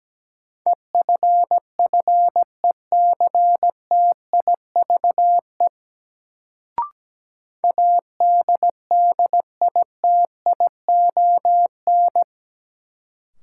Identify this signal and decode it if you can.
Morse code, heard as a keyed tone, about 17 wpm, 700 Hz. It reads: EFFECTIVE ADDITION
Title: Sentences from Top 1000 Words - Review Part 6 - New Word Drill - 17wpm